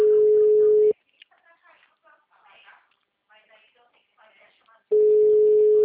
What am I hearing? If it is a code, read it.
Ringback tone (Europe, 425 Hz), with speech.
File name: out- -2005- - - .14889